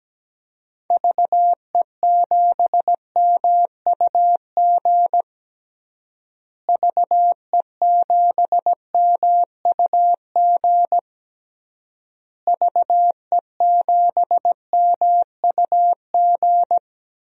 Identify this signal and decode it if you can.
Morse code, heard as a keyed tone, about 17 wpm, 700 Hz. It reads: VE7MUG VE7MUG VE7MUG
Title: DXCC-100 Call Signs - 6 Characters - Repeated 3x - 17wpm